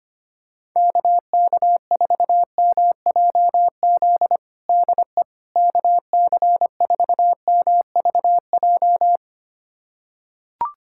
Morse code, heard as a keyed tone, about 25 wpm, 700 Hz. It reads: KK4MJZ DE KC4MVJ
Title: USA Callsign DE Callsign - 6 Characters - 25wpm